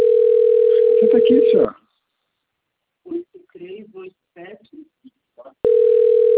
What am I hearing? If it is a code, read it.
Ringback tone (North America), 440 + 480 Hz, with speech.
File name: out- - - - - .33325